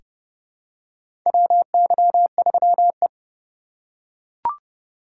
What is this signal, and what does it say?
Morse code, heard as a keyed tone, about 30 wpm, 700 Hz. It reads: WY3E
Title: USA Call Signs - 4 Characters - 6x Character Spacing - 30wpm